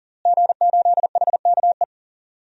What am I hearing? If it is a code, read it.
Morse code, heard as a keyed tone, about 40 wpm, 700 Hz. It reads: G8HKE
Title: DXCC-100 Call Signs - 5 Characters - Repeated 3x - 40wpm